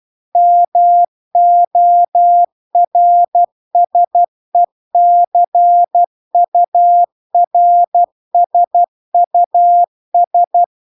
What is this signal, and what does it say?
Morse code, heard as a keyed tone, about 12 wpm, 700 Hz. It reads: MORSECURSUS
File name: cw-cursus-VERON-a03